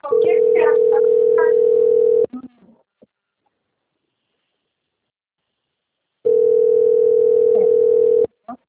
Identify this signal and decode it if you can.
Ringback tone (North America), 440 + 480 Hz.
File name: out- - - - - .88676